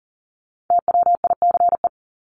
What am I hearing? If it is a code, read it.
Morse code, heard as a keyed tone, about 40 wpm, 700 Hz. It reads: TWICE
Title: Top 1501-1750 Words 40wpm